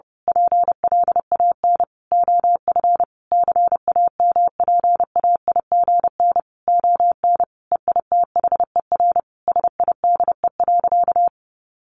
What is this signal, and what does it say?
Morse code, heard as a keyed tone, about 30 wpm, 700 Hz. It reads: PLAN OF CAMPAIGN ON EITHER SIDE.
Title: Sentences from Top 1200 Words 30wpm